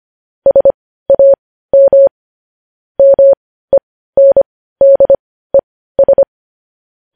Morse code, heard as a keyed tone, about 25 wpm, spaced as for 15 wpm, 550 Hz. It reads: SAM MENDES
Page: news-headlines-in-morse-code-at-15-wpm